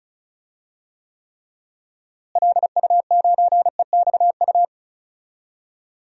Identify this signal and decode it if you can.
Morse code, heard as a keyed tone, about 36 wpm, 700 Hz. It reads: LU9EXU
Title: DXCC-100 Call Signs - 6 Characters - Repeated 3x - 35wpm